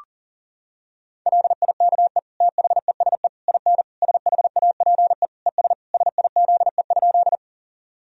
Morse code, heard as a keyed tone, about 37 wpm, 700 Hz. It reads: LIKE THESE IN SHAPE ES SIZE?
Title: Sentences from Top 500 Words and Abbreviations 40wpm